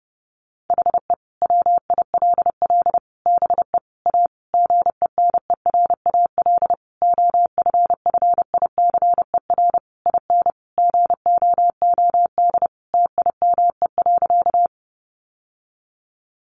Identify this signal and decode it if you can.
Morse code, heard as a keyed tone, about 28 wpm, 700 Hz. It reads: HE WILL BE A GENERAL OFFICER IN GOOD TIME.
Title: Sentences from Top 1200 Words 30wpm